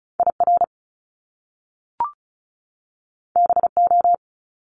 Morse code, heard as a keyed tone, about 36 wpm, 700 Hz. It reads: IR BO
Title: Top Two Letter Combinations in Words 35wpm